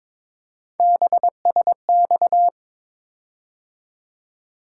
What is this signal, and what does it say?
Morse code, heard as a keyed tone, about 22 wpm, 700 Hz. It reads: BSX
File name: practice / 22wpm / Three Letter-Number Combinations - 22wpm